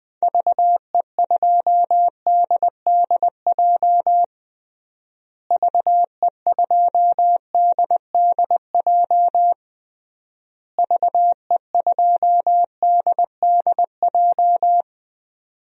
Morse code, heard as a keyed tone, about 20 wpm, 700 Hz. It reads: VE2DDJ VE2DDJ VE2DDJ
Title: DXCC-100 Call Signs - 6 Characters - Repeated 3x - 20wpm